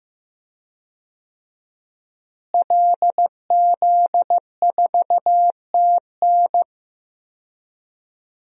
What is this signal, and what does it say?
Morse code, heard as a keyed tone, about 15 wpm, 700 Hz. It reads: LZ4TN
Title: DXCC-100 Call Signs - 5 Characters - Repeated 3x - 15wpm